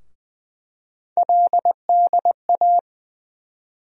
Morse code, heard as a keyed tone, about 20 wpm, 700 Hz. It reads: LDA
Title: USA Call Sign Postfixes - 3 Letters 20wpm